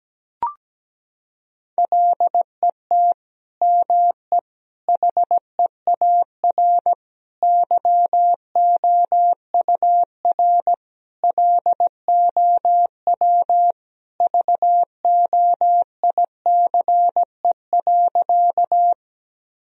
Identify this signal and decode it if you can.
Morse code, heard as a keyed tone, about 17 wpm, 700 Hz. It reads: LET ME HEAR YOUR LOW VOICE.
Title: Sentences from Top 1000 Words - Review Part 1 - 17wpm